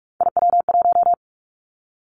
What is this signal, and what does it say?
Morse code, heard as a keyed tone, about 46 wpm, 700 Hz. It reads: IW1
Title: Callsign Prefixes for Top 25 DXCC - 45wpm